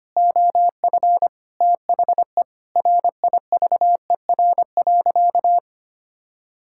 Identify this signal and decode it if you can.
Morse code, heard as a keyed tone, about 25 wpm, 700 Hz. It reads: OF THE RIVER.
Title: Sentences from Top 1000 Words - Review Part 5 - 25wpm